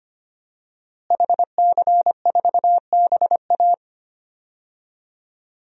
Morse code, heard as a keyed tone, about 25 wpm, 700 Hz. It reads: HC4BA
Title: DXCC-100 Call Signs - 5 Characters - Repeated 3x - 25wpm